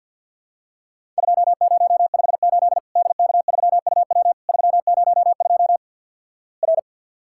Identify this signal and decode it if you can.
Morse code, heard as a keyed tone, about 50 wpm, 700 Hz. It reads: 2058 DK3UW 302 R
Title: Worked All Europe DX Contest - QTC Exchange with Repeats - 50wpm